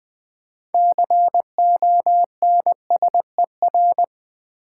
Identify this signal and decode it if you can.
Morse code, heard as a keyed tone, about 20 wpm, 700 Hz. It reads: CONSER
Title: Build A Word - 6 Letters At A Time - 12-Letter Words - 20wpm